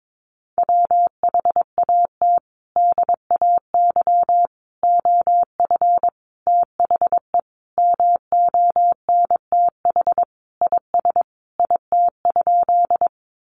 Morse code, heard as a keyed tone, about 22 wpm, 700 Hz. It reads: WHAT DAY OF THE MONTH IS IT?
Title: Sentences from Top 800 Words 22wpm